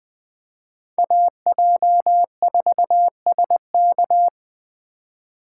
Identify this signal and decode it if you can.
Morse code, heard as a keyed tone, about 20 wpm, 700 Hz. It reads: AJ4SK